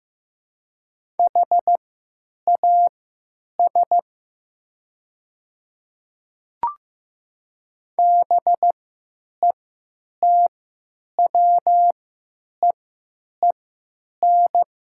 Morse code, heard as a keyed tone, about 15 wpm, spaced as for 9 wpm, 700 Hz. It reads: HAS BETWEEN
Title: 3x Character Spacing - Top 300 Words - 15wpm